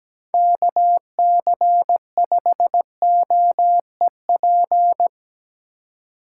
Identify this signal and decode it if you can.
Morse code, heard as a keyed tone, about 18 wpm, 700 Hz. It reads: KC5OEP
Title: Speed Racing - USA 2x3 Call Signs 17wpm